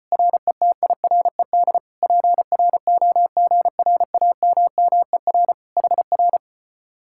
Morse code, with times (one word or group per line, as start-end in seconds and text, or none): 0.1-1.8 RETIRED
2.0-5.5 PROGRAMMER
5.8-6.4 HR